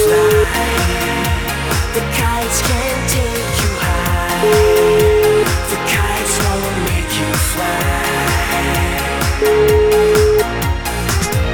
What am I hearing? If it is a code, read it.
Ringback tone (Europe, 425 Hz), with noise.